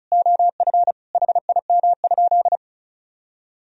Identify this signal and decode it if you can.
Morse code, heard as a keyed tone, about 36 wpm, 700 Hz. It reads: OF HIM?
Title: Sentences from Top 100 Words and Abbreviations 35wpm